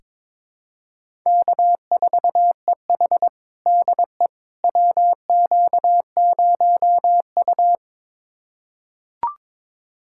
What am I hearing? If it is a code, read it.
Morse code, heard as a keyed tone, about 22 wpm, 700 Hz. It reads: K4EH DE WQ0U
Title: USA Callsign DE Callsign - 4 Characters - 22wpm